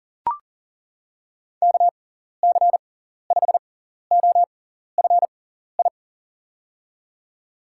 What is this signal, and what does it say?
Morse code, heard as a keyed tone, about 40 wpm, spaced as for 14 wpm, 700 Hz. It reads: KC5OFI
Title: USA Call Signs - 6 Characters - 6x Character Spacing - 40wpm